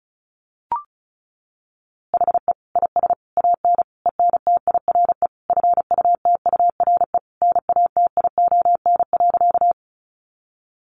Morse code, heard as a keyed tone, about 34 wpm, 700 Hz. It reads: HE IS AN ENTIRE FUTURE NATION.
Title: A Sound Of Thunder 35wpm